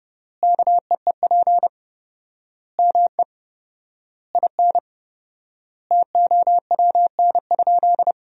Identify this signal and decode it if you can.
Morse code, heard as a keyed tone, about 30 wpm, 700 Hz. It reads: KEEP ME IN TOWN?
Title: 4x Word-Spacing - Sentences from Top 500 Words - 30wpm